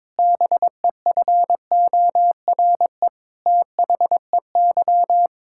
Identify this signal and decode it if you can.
Morse code, heard as a keyed tone, about 22 wpm, 700 Hz. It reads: BEFORE THEY
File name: Sentences with 7-Word Phrases - 22wpm